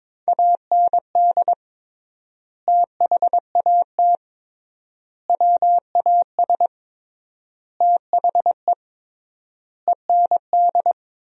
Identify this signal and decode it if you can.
Morse code, heard as a keyed tone, about 22 wpm, 700 Hz. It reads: AND THAT WAS THE END
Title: 3x Word-Spacing - Sentences from Top 200 Words - 22wpm